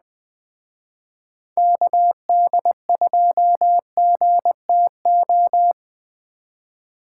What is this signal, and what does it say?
Morse code, heard as a keyed tone, about 20 wpm, 700 Hz. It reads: KD2GTO
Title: Speed Racing - USA 2x3 Call Signs 17wpm